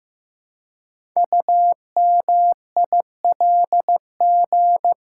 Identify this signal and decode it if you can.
Morse code, heard as a keyed tone, about 15 wpm, 700 Hz. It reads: UMILG
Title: Code Groups - 5 Letters - 15wpm